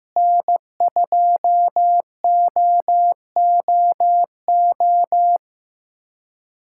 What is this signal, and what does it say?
Morse code, heard as a keyed tone, about 15 wpm, 700 Hz. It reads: N2OOO
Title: USA Callsign DE Callsign - 5 Characters - 15wpm